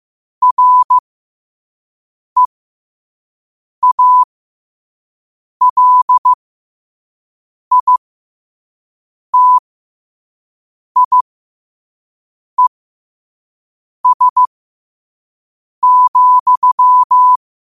Morse code, heard as a keyed tone, about 15 wpm, spaced as for 5 wpm, 1000 Hz. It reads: REALITIES,
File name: qotdfiles / 5wpm